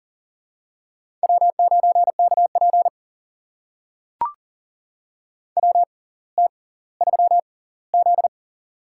Morse code, heard as a keyed tone, about 38 wpm, spaced as for 20 wpm, 700 Hz. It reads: W9KP WT3Z
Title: USA Call Signs - 4 Characters - 6x Character Spacing - 40wpm